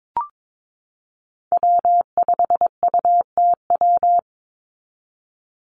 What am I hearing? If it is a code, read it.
Morse code, heard as a keyed tone, about 22 wpm, 700 Hz. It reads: W5UTW
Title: Call Signs Repeated 3x - 22wpm